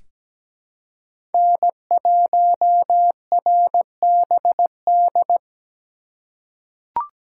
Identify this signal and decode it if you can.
Morse code, heard as a keyed tone, about 17 wpm, 700 Hz. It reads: N1RBD
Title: K1USN SST QSO Practice - 17wpm